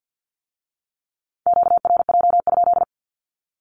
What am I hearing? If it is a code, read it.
Morse code, heard as a keyed tone, about 50 wpm, 700 Hz. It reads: QRJ?
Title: Q-Codes with Spoken Meaning - 50wpm